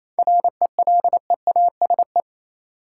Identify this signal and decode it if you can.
Morse code, heard as a keyed tone, about 28 wpm, 700 Hz. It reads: RELEASE
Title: Sentences from Top 1100 Words - New Word Drill Alt - 28wpm